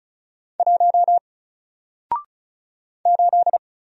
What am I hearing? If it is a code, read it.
Morse code, heard as a keyed tone, about 36 wpm, 700 Hz. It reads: 1 8
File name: Single-Letter-Number-35wpm